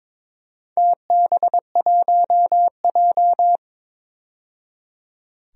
Morse code, heard as a keyed tone, about 22 wpm, 700 Hz. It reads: TB1J
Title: DXCC-100 Call Signs - 4 Characters - 22wpm